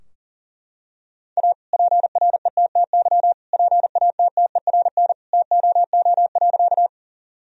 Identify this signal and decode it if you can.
Morse code, heard as a keyed tone, about 40 wpm, 700 Hz. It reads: A PRETTY PATTERN TOO.
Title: Sentences from Top 1000 Words - Review Part 2 - 40wpm